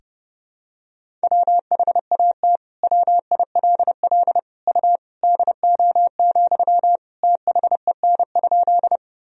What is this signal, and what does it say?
Morse code, heard as a keyed tone, about 32 wpm, 700 Hz. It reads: WHAT WILL U DO, THEN?
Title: Sentences from Top 100 Words and Abbreviations 30wpm